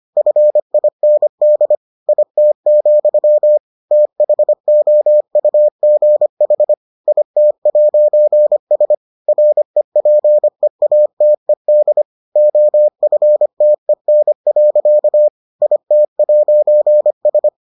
Morse code, heard as a keyed tone, about 25 wpm, 600 Hz. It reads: FIND IT, THOUGH IT'S REPEATED OFTEN. IT'S